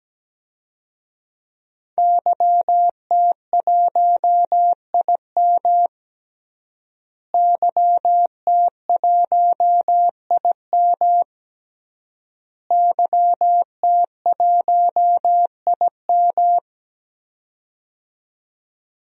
Morse code, heard as a keyed tone, about 17 wpm, 700 Hz. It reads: YT1IM YT1IM YT1IM
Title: DXCC-100 Call Signs - 5 Characters - Repeated 3x - 17wpm